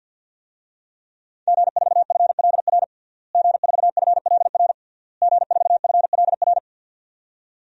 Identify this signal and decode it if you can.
Morse code, heard as a keyed tone, about 48 wpm, 700 Hz. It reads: G4FLR G4FLR G4FLR
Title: DXCC-100 Call Signs - 5 Characters - Repeated 3x - 50wpm